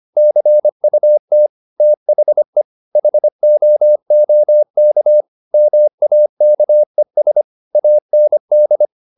Morse code, heard as a keyed tone, about 25 wpm, 600 Hz. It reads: CUT THE HOOK MAKES AND
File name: the old man and the sea part 1 25wpm